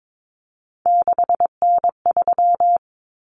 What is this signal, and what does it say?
Morse code, heard as a keyed tone, about 22 wpm, 700 Hz. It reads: 6N3
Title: Common Cut Numbers - 3-Digit 22wpm